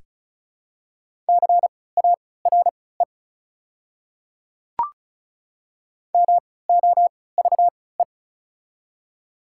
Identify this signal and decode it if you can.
Morse code, heard as a keyed tone, about 36 wpm, spaced as for 20 wpm, 700 Hz. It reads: CARE MOVE
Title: 3x Character Spacing - Top 300 Words - 35wpm